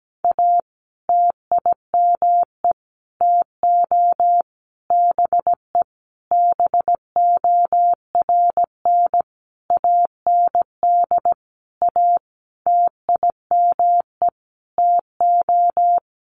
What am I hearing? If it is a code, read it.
Morse code, heard as a keyed tone, about 17 wpm, 700 Hz. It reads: A TIME TO BE BORN AND A TIME TO
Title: Sentences from Top 1300 Words 17wpm